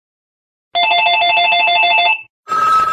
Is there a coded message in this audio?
No.